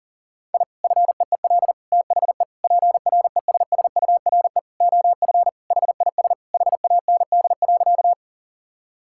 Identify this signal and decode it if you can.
Morse code, heard as a keyed tone, about 40 wpm, 700 Hz. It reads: I FEEL THE PRESSURE OF HIS HAND.
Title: Sentences from Top 1000 Words - Review Part 6 - 40wpm